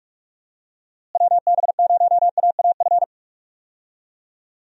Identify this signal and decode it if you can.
Morse code, heard as a keyed tone, about 47 wpm, 700 Hz. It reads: WB0AAF
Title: USA Call Signs - 6 Characters - Repeated 3x - 45wpm